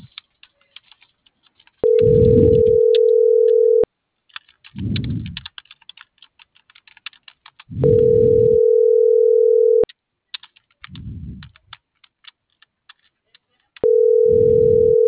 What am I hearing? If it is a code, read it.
Ringback tone (North America), 440 + 480 Hz.